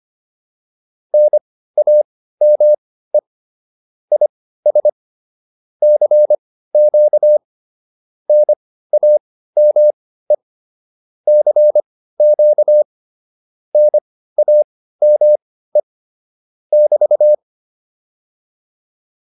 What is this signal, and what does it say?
Morse code, heard as a keyed tone, about 25 wpm, spaced as for 15 wpm, 600 Hz. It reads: NAME IS CQ NAME CQ NAME <BT>